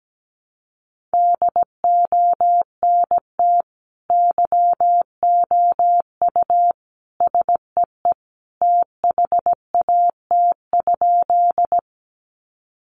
Morse code, heard as a keyed tone, about 17 wpm, 700 Hz. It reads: DONT YOU SEE THAT?
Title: Sentences from Top 300 Words 17wpm